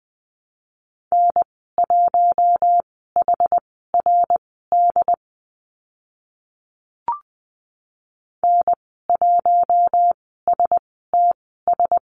Morse code, heard as a keyed tone, about 20 wpm, spaced as for 14 wpm, 700 Hz. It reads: N1HRD N1STS
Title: 069 Lesson 12g - Letter d - Call Signs 20wpm